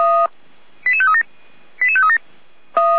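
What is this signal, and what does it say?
Five-tone selective call (ZVEI), heard as quick runs of five tones, tones of about 70 ms. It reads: 88537 88537
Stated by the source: ZVEI five-tone selective call 88119, received by SDR.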